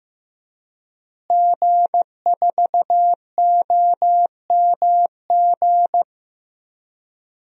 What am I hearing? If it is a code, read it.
Morse code, heard as a keyed tone, about 15 wpm, 700 Hz. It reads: G4OMG
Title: DXCC-100 Call Signs - 5 Characters - Repeated 3x - 15wpm